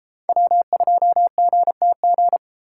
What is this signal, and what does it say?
Morse code, heard as a keyed tone, about 29 wpm, 700 Hz. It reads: W2GTG